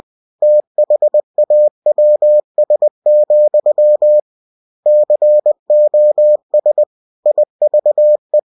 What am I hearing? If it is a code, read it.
Morse code, heard as a keyed tone, about 20 wpm, 600 Hz. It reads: THAWS, COS IVE